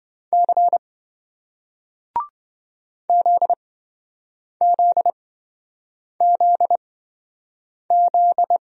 Morse code, heard as a keyed tone, about 27 wpm, spaced as for 7 wpm, 700 Hz. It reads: C ZZZZ